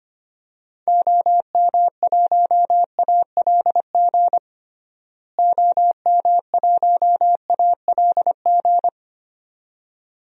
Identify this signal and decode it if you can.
Morse code, heard as a keyed tone, about 25 wpm, 700 Hz. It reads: OM1ALG OM1ALG
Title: DXCC-100 Call Signs - 6 Characters - Repeated 2x - 25wpm